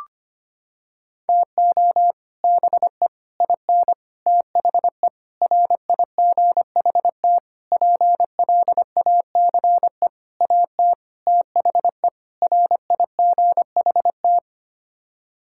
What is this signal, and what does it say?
Morse code, heard as a keyed tone, about 25 wpm, 700 Hz. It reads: TO BE IN THE RIGHT PLACE AT THE RIGHT
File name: Sets of 9 Words - 25wpm